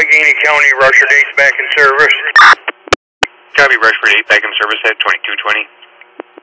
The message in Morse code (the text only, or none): none